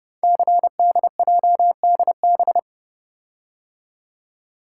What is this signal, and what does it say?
Morse code, heard as a keyed tone, about 28 wpm, 700 Hz. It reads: CDJDB
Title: Code Groups - 5 Letters - 30wpm